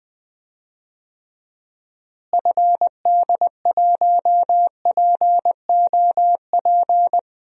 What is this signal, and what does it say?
Morse code, heard as a keyed tone, about 20 wpm, 700 Hz. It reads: FD1POP
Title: DXCC-100 Call Signs - 6 Characters - Repeated 2x - 20wpm